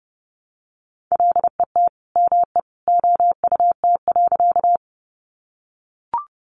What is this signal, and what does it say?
Morse code, heard as a keyed tone, about 30 wpm, 700 Hz. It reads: LET ME OUT.
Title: Sentences from Top 300 Words 30wpm